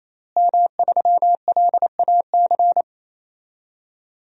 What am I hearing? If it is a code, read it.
Morse code, heard as a keyed tone, about 28 wpm, 700 Hz. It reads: M3LAC
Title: DXCC-100 Call Signs - 5 Characters - Repeated 3x - 28wpm